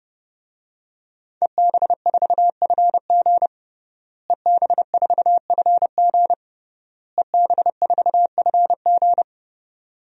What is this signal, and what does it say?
Morse code, heard as a keyed tone, about 30 wpm, 700 Hz. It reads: EB4FG EB4FG EB4FG
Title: DXCC-100 Call Signs - 5 Characters - Repeated 3x - 30wpm